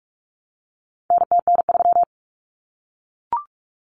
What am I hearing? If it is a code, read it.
Morse code, heard as a keyed tone, about 44 wpm, 700 Hz. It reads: NTN3
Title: Common Cut Numbers - 4-Digit 45wpm